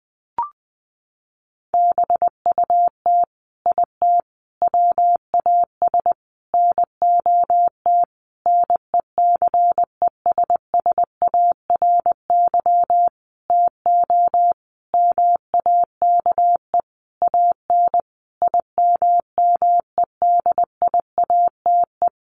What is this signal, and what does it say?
Morse code, heard as a keyed tone, about 20 wpm, 700 Hz. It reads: BUT IT WAS NOT NECESSARY TO MAKE AN IMMEDIATE
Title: Sentences from Top 1200 Words 20wpm